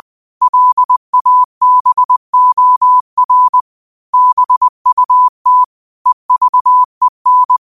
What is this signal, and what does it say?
Morse code, heard as a keyed tone, about 20 wpm, 1000 Hz. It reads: LABOR BUT EVEN
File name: qotdfiles / 20wpm